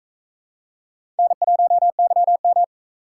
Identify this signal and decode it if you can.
Morse code, heard as a keyed tone, about 43 wpm, 700 Hz. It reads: N1YM